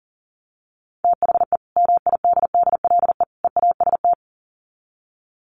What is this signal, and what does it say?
Morse code, heard as a keyed tone, about 40 wpm, 700 Hz. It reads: THE MIDDLE EAST
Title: Sets of 3 Words 40wpm